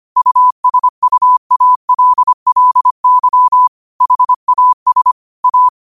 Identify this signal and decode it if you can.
Morse code, heard as a keyed tone, about 25 wpm, 1000 Hz. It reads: USUALLY HAS A
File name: qotdfiles / 25wpm